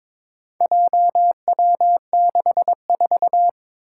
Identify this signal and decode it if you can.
Morse code, heard as a keyed tone, about 21 wpm, 700 Hz. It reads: JW64
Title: Four Challenging Letter-Number Combinations 22wpm